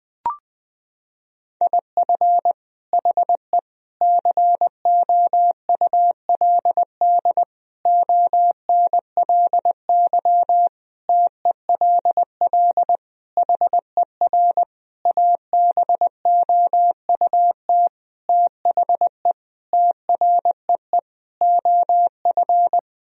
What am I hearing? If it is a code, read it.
Morse code, heard as a keyed tone, about 20 wpm, 700 Hz. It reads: IF HE COULD ONLY TELL HER ABOUT THE TREE OF
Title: Sentences from Top 1000 Words - Review Part 1 - 20wpm